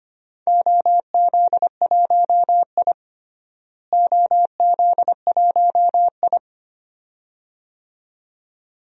Morse code, heard as a keyed tone, about 25 wpm, 700 Hz. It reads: OZ1I OZ1I
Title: DXCC-100 Call Signs - 4 Characters - Repeated 2x - 25wpm